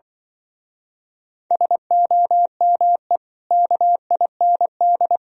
Morse code, heard as a keyed tone, about 25 wpm, 700 Hz. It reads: SOME KIND
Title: Speed-Racing - Sets of 3 Words 20wpm